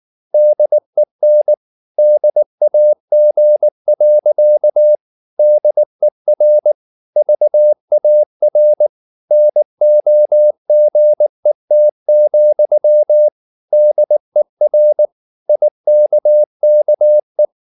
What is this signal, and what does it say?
Morse code, heard as a keyed tone, about 19 wpm, 600 Hz. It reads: DEN DAG. DER VAR NOGET, DER IKKE
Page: Ferie 19 wpm